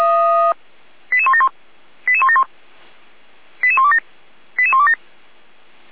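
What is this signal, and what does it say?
Five-tone selective call (ZVEI), heard as quick runs of five tones, tones of about 70 ms. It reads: 88171 88171 88127 88127
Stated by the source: ZVEI five-tone call 88199 received via SDR, fire brigade Feuerwehr Orlamünde.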